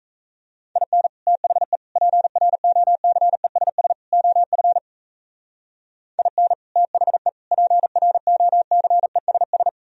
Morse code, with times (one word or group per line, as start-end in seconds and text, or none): 0.8-1.1 IN
1.3-1.8 THE
2.0-3.9 PROCESS
4.1-4.8 OF
6.2-6.5 IN
6.8-7.3 THE
7.5-9.7 PROCESS